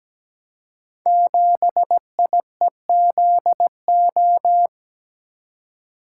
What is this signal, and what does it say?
Morse code, heard as a keyed tone, about 17 wpm, 700 Hz. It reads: 7IEZO
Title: Code Groups - 5 Letter:Numbers - 17wpm